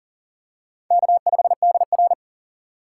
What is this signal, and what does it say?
Morse code, heard as a keyed tone, about 40 wpm, 700 Hz. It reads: K5DR
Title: USA Call Signs - 4 Characters - 4x Character Spacing - 40wpm